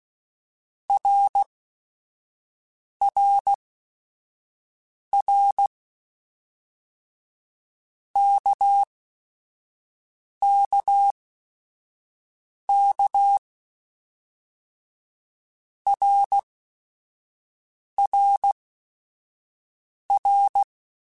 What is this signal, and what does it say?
Morse code, heard as a keyed tone, about 16 wpm, spaced as for 5 wpm, 800 Hz. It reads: RRR KKK RRR
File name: Basic Sets / Basic Letter Groups Collective7 / RKPX@5